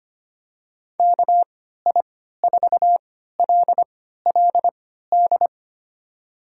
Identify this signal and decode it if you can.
Morse code, heard as a keyed tone, about 25 wpm, spaced as for 14 wpm, 700 Hz. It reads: KI4LLD
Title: USA Call Signs - 6 Characters - 3x Character Spacing - 25wpm